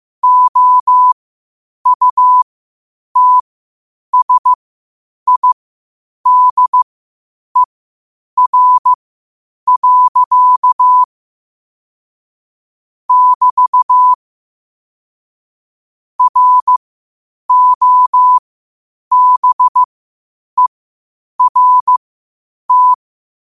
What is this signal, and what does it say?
Morse code, heard as a keyed tone, about 15 wpm, spaced as for 8 wpm, 1000 Hz. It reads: OUTSIDER. <BT> ROBERT
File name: qotdfiles / 8wpm